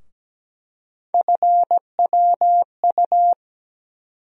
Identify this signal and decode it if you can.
Morse code, heard as a keyed tone, about 17 wpm, 700 Hz. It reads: FWU
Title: USA Call Sign Postfixes - 3 Letters 17wpm